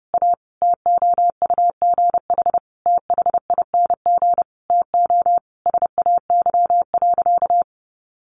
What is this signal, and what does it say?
Morse code, heard as a keyed tone, about 30 wpm, 700 Hz. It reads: A TOUGH THING TO SAY.
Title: Sentences from Top 1200 Words 30wpm